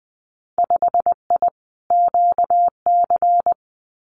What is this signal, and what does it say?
Morse code, heard as a keyed tone, about 20 wpm, 700 Hz. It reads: 5I QC
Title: Winter Field Day Exchanges Encore 20wpm